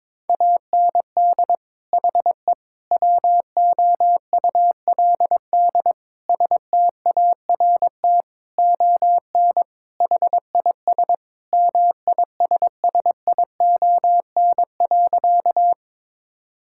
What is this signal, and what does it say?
Morse code, heard as a keyed tone, about 22 wpm, 700 Hz. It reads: AND HE WOULD START ON HIS MISSION.
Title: Sentences from Top 1600 Words 22wpm